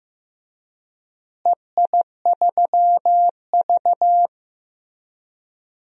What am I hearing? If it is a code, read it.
Morse code, heard as a keyed tone, about 15 wpm, 700 Hz. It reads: EI3V
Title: DXCC-100 Call Signs - 4 Characters - Repeated 3x - 15wpm